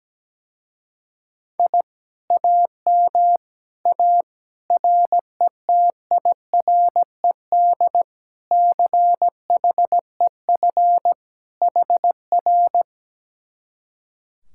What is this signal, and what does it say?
Morse code, heard as a keyed tone, about 17 wpm, 700 Hz. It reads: I AM A RETIRED CHEF HR
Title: Speed-Racing Common QSO Phrases - 17wpm